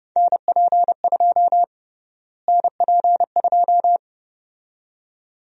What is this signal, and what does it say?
Morse code, heard as a keyed tone, about 32 wpm, 700 Hz. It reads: NP2 NP2
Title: USA Call Sign Prefixes - Repeated 2x - 30wpm